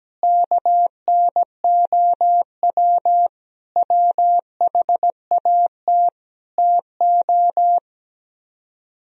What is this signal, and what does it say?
Morse code, heard as a keyed tone, about 17 wpm, 700 Hz. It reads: KNOW WHAT TO
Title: Speed-Racing - Sets of 5 Words 17wpm